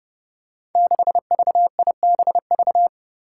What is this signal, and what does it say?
Morse code, heard as a keyed tone, about 29 wpm, 700 Hz. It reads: 6VIBV
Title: Five Challenging Letter-Number Combinations 30wpm